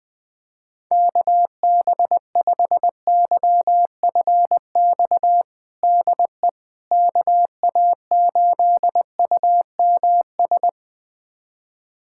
Morse code, heard as a keyed tone, about 20 wpm, 700 Hz. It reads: KB5YFX DE KA8UMS
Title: USA Callsign DE Callsign - 6 Characters - 20wpm